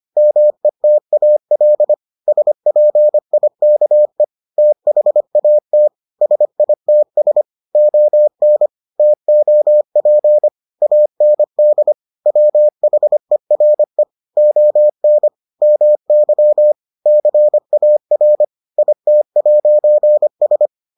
Morse code, heard as a keyed tone, about 25 wpm, 600 Hz. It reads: METAL SPIKE THAT SITS ON TOP AND WHERE ON MY CAR IT'S